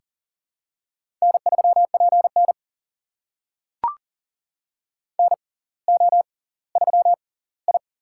Morse code, heard as a keyed tone, about 40 wpm, spaced as for 20 wpm, 700 Hz. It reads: N3PN NO3I
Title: USA Call Signs - 4 Characters - 6x Character Spacing - 40wpm